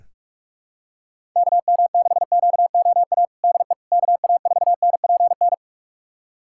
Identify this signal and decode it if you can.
Morse code, heard as a keyed tone, about 43 wpm, 700 Hz. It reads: KM6QOA DE KA4NPN
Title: USA Callsign DE Callsign - 6 Characters - 45wpm